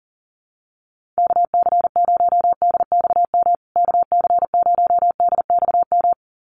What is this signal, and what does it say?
Morse code, heard as a keyed tone, about 40 wpm, 700 Hz. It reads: KC0DXM KC0DXM
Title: USA Call Signs Repeated 2x 40wpm